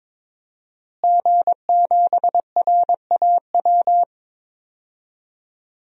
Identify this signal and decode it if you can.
Morse code, heard as a keyed tone, about 22 wpm, 700 Hz. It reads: G7RAW
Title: DXCC-100 Call Signs - 5 Characters - Repeated 3x - 22wpm